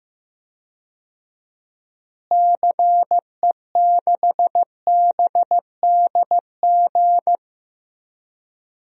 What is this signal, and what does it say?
Morse code, heard as a keyed tone, about 15 wpm, 700 Hz. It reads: CE6BDG
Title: DXCC-100 Call Signs - 6 Characters - Repeated 3x - 15wpm